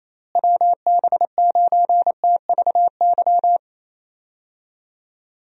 Morse code, heard as a keyed tone, about 28 wpm, 700 Hz. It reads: WB9TVY
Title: USA Call Signs - 6 Characters - Repeated 2x - 28wpm